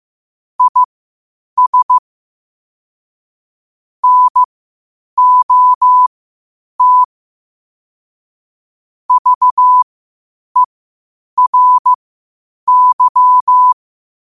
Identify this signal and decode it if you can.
Morse code, heard as a keyed tone, about 15 wpm, spaced as for 8 wpm, 1000 Hz. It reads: IS NOT VERY
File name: qotdfiles / 8wpm